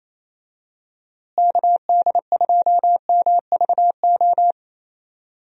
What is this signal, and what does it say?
Morse code, heard as a keyed tone, about 28 wpm, 700 Hz. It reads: KD2MVO
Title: USA Call Signs - 6 Characters - Repeated 2x - 28wpm